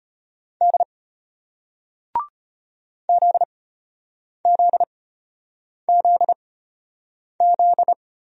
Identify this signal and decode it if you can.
Morse code, heard as a keyed tone, about 34 wpm, spaced as for 8 wpm, 700 Hz. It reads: D ZZZZ